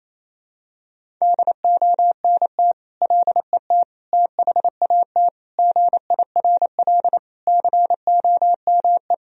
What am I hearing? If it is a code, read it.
Morse code, heard as a keyed tone, about 28 wpm, 700 Hz. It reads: DONT LET THAT GIRL COME
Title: Sentences from Top 1000 Words - Review Part 1 - 28wpm